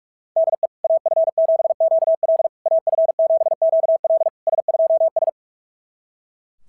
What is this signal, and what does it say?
Morse code, heard as a keyed tone, about 44 wpm, 690 Hz. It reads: DE AF7QL AF7QL S2S
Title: SOTA QSO Practice 45wpm